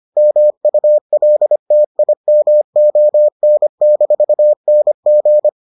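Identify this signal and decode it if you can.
Morse code, heard as a keyed tone, about 25 wpm, 600 Hz. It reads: MULTIMON-NG
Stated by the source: Morse edition of the Foundations of Amateur Radio episodes.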